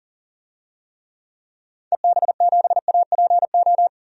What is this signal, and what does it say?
Morse code, heard as a keyed tone, about 40 wpm, 700 Hz. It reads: EB7APO
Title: DXCC-100 Call Signs - 6 Characters - 40wpm